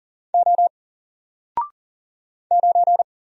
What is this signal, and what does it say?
Morse code, heard as a keyed tone, about 40 wpm, 700 Hz. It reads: O 9
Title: Single Letter-Number Alt 40wpm